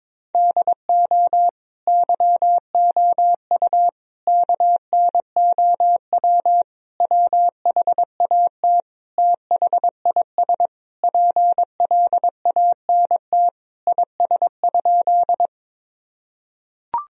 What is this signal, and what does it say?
Morse code, heard as a keyed tone, about 21 wpm, 700 Hz. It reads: DO YOU KNOW WHAT THIS PLANT IS?
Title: Sentences from Top 1000 Words - Review Part 1 - 22wpm